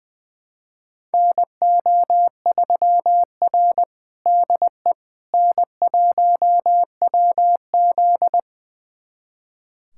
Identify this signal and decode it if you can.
Morse code, heard as a keyed tone, about 21 wpm, 700 Hz. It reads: NO3R DE N1WZ
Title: USA Callsign DE Callsign - 4 Characters - 20wpm